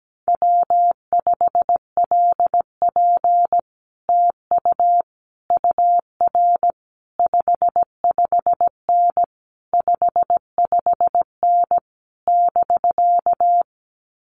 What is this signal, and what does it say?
Morse code, heard as a keyed tone, about 17 wpm, 700 Hz. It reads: W5LP TU UR 55N 55N <BK>
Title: Common QSO Phrases Encore 17wpm